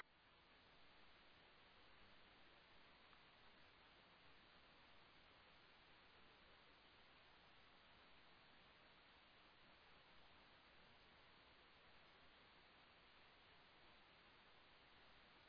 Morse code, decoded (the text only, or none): none